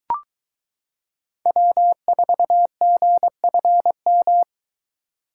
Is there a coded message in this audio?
Yes: Morse code from 1.5 to 4.4 s, heard as a keyed tone, about 22 wpm, 700 Hz. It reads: W4GFM